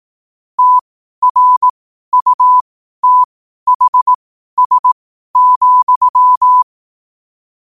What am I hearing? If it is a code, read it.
Morse code, heard as a keyed tone, about 18 wpm, spaced as for 12 wpm, 1000 Hz. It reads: TRUTHS,
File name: qotdfiles / 12wpm